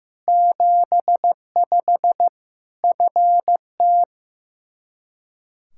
Morse code, heard as a keyed tone, about 15 wpm, 700 Hz. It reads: 75 FT
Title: Practice QSO - Exchange 2 - Info - 15wpm